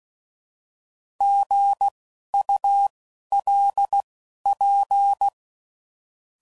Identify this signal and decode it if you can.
Morse code, heard as a keyed tone, about 16 wpm, spaced as for 11 wpm, 800 Hz. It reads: GULP
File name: Prefixes / By First Letter